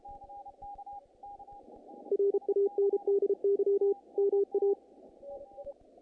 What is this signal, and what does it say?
Morse code, heard as a keyed tone, about 33 wpm, 400 Hz. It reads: RANDY MA
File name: contests / naqp-cw-k5tr-op-k5pi- -2300